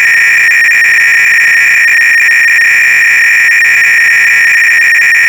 300-baud modem data.